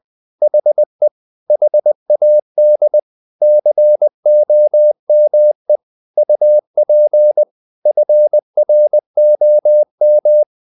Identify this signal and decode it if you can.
Morse code, heard as a keyed tone, about 20 wpm, 600 Hz. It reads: HE HAD COME UP FROM
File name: the old man and the sea part 2 20wpm